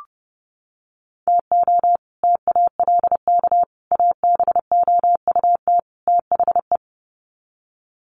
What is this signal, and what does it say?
Morse code, heard as a keyed tone, about 30 wpm, 700 Hz. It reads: TO TALK ABOUT THE
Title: Sets of 4 Words 30wpm